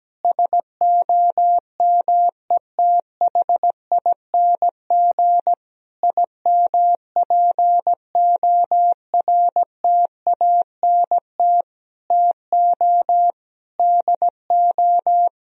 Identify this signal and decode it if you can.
Morse code, heard as a keyed tone, about 17 wpm, 700 Hz. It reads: SOMETHING IMPORTANT TO DO
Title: Sentences from Top 1000 Words - Review Part 4 - 17wpm